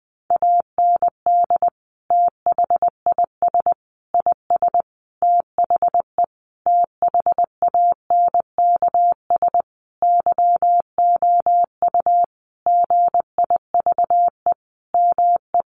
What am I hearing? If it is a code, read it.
Morse code, heard as a keyed tone, about 20 wpm, 700 Hz. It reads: AND THIS IS THE THANKS YOU GIVE ME
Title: Sentences from Top 1200 Words 20wpm